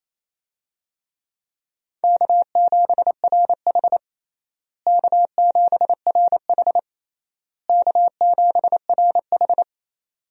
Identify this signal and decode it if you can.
Morse code, heard as a keyed tone, about 28 wpm, 700 Hz. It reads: K7RH K7RH K7RH
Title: USA Call Signs - 4 Characters - Repeated 3x - 28wpm